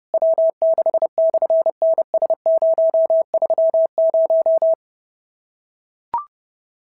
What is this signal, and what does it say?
Morse code, heard as a keyed tone, about 30 wpm, 650 Hz. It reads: W6/NS030
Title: SOTA QSO Practice 30wpm